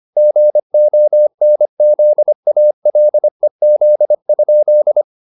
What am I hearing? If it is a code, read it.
Morse code, heard as a keyed tone, about 25 wpm, 600 Hz. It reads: GONZALEZ?
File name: the old man and the sea part 1 25wpm